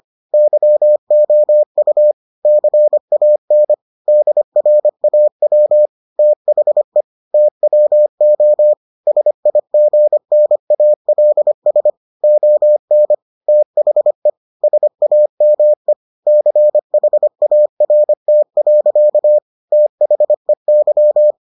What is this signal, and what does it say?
Morse code, heard as a keyed tone, about 25 wpm, 600 Hz. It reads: YOU CAN DRAW THE TWO SIGNALS ON THE SAME CHART. THEY